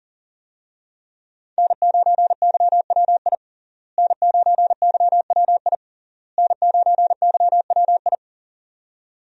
Morse code, heard as a keyed tone, about 37 wpm, 700 Hz. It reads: N9YWI N9YWI N9YWI
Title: USA Call Signs - 5 Characters - Repeated 3x - 40wpm